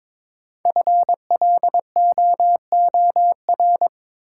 Morse code, heard as a keyed tone, about 22 wpm, 700 Hz. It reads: FLOOR
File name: 5 Letter Words - First 3 Letters Given - 22wpm